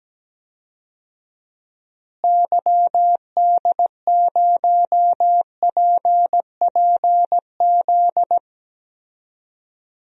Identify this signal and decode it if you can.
Morse code, heard as a keyed tone, about 17 wpm, 700 Hz. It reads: YD0PPZ
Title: DXCC-100 Call Signs - 6 Characters - Repeated 3x - 17wpm